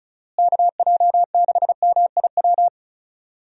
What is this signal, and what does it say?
Morse code, heard as a keyed tone, about 34 wpm, 700 Hz. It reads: KJ6MIW
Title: USA Call Signs - 6 Characters - Repeated 2x - 35wpm